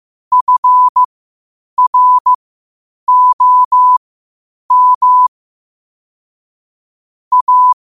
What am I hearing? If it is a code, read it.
Morse code, heard as a keyed tone, about 15 wpm, spaced as for 8 wpm, 1000 Hz. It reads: FROM A
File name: qotdfiles / 8wpm